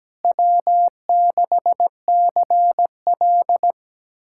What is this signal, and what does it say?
Morse code, heard as a keyed tone, about 17 wpm, 700 Hz. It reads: W6CL
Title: K1USN SST QSO Practice - 17wpm